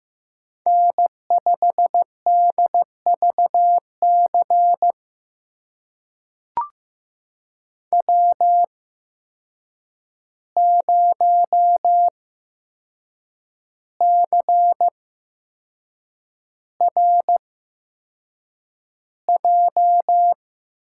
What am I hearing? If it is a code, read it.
Morse code, heard as a keyed tone, about 15 wpm, spaced as for 6 wpm, 700 Hz. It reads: N5DVC W0CRJ
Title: USA Call Signs - 5 Characters - 8x Character Spacing - 15wpm